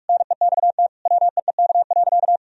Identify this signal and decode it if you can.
Morse code, heard as a keyed tone, about 46 wpm, 700 Hz. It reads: NEXT WEEK.
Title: Sentences from Top 1000 Words - Review Part 2 - 45wpm